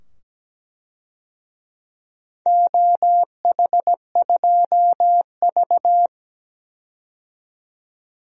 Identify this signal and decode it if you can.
Morse code, heard as a keyed tone, about 17 wpm, 700 Hz. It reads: OH2V
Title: DXCC-100 Call Signs - 4 Characters - 17wpm